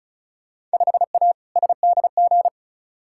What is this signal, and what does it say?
Morse code, heard as a keyed tone, about 34 wpm, 700 Hz. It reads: 5A SDG
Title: Field Day Exchanges Encore - 35wpm